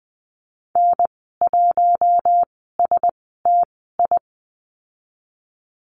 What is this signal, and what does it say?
Morse code, heard as a keyed tone, about 20 wpm, spaced as for 14 wpm, 700 Hz. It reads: N1STI
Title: 042 Lesson 8g - Number 1 - Call Signs 20wpm